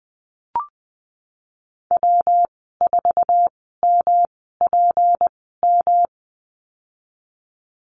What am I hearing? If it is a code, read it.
Morse code, heard as a keyed tone, about 19 wpm, spaced as for 14 wpm, 700 Hz. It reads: W4MPM
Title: 186 Lesson 30g - Letter b - Call Signs 20wpm